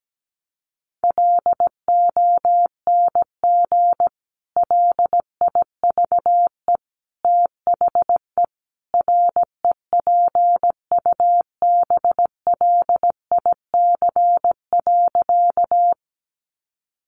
Morse code, heard as a keyed tone, about 17 wpm, 700 Hz. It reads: LONG LIVE THE REPUBLIC.
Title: Sentences from Top 1500 Words 17wpm